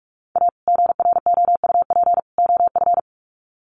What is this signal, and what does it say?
Morse code, heard as a keyed tone, about 44 wpm, 700 Hz. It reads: A GROUP OF
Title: Sets of 3 Words 45wpm